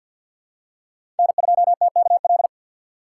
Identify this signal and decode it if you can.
Morse code, heard as a keyed tone, about 53 wpm, 690 Hz. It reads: N2TKL
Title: USA Call Signs - 5 Characters - 6x Character Spacing - 50wpm